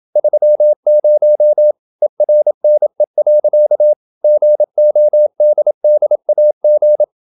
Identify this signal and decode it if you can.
Morse code, heard as a keyed tone, about 27 wpm, 600 Hz. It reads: 30 ERNE. GODDAG
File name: Ferie QRQ 27wpm